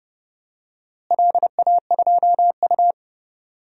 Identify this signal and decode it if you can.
Morse code, heard as a keyed tone, about 30 wpm, 700 Hz. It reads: LA2U
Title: DXCC-100 Call Signs - 4 Characters - Repeated 2x - 30wpm